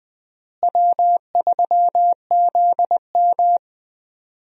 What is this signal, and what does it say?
Morse code, heard as a keyed tone, about 20 wpm, 700 Hz. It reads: W3ZM
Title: USA Call Signs - 4 Characters - Repeated 3x - 20wpm